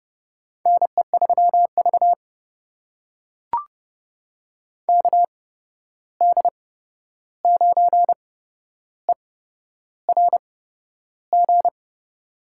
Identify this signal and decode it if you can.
Morse code, heard as a keyed tone, about 30 wpm, spaced as for 11 wpm, 700 Hz. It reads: NE3V KD9ERG
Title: USA Call Signs - 8x Character Spacing - 30wpm